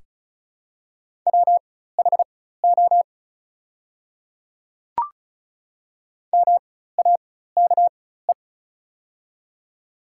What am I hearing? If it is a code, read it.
Morse code, heard as a keyed tone, about 36 wpm, spaced as for 16 wpm, 690 Hz. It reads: WHO MAKE
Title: 4x Character Spacing - Top 100 Words - 35wpm